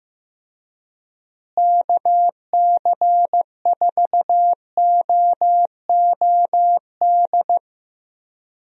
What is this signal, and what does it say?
Morse code, heard as a keyed tone, about 15 wpm, 700 Hz. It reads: KC4OOD
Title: USA Call Signs - 6 Characters - Repeated 3x - 15wpm